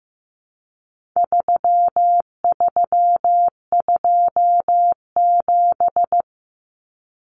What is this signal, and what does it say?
Morse code, heard as a keyed tone, about 15 wpm, 700 Hz. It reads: 3327
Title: Common Cut Numbers - 4-Digit 15wpm